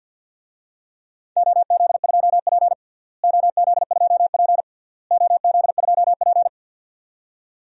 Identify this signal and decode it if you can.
Morse code, heard as a keyed tone, about 50 wpm, 700 Hz. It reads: OZ2P OZ2P OZ2P
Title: DXCC-100 Call Signs - 4 Characters - Repeated 3x - 50wpm